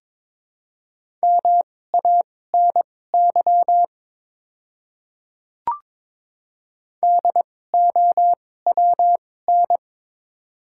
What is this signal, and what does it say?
Morse code, heard as a keyed tone, about 21 wpm, spaced as for 16 wpm, 700 Hz. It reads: MANY DOWN
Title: 2x Character Spacing - Top 100 Words - 22wpm